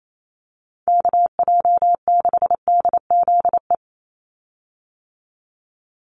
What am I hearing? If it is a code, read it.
Morse code, heard as a keyed tone, about 28 wpm, 700 Hz. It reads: KJ6DZE
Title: USA Call Signs 28wpm